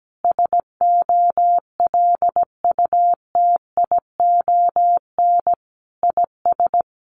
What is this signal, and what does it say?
Morse code, heard as a keyed tone, about 17 wpm, 700 Hz. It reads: SOLUTION IS
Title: Sentences from Top 1300 Words 17wpm